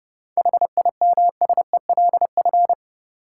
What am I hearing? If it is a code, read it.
Morse code, heard as a keyed tone, about 30 wpm, 700 Hz. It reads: HIMSELF
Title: Sentences from Top 1500 Words - New Word Drill Alt - 30wpm